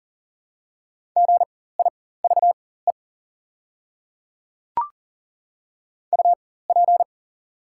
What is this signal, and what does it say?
Morse code, heard as a keyed tone, about 40 wpm, spaced as for 19 wpm, 700 Hz. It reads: GIVE UP
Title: 4x Character Spacing - Top 200 Words - 40wpm